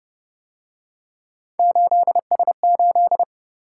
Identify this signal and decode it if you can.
Morse code, heard as a keyed tone, about 29 wpm, 700 Hz. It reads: 8S8